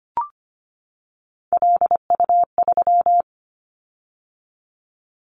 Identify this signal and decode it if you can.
Morse code, heard as a keyed tone, about 25 wpm, 690 Hz. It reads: LU3
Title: Callsign Prefixes for Top 50 DXCC - 25wpm